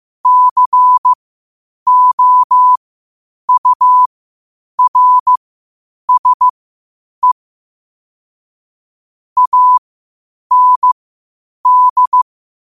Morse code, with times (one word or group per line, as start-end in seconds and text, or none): 0.3-7.3 COURSE
9.4-12.2 AND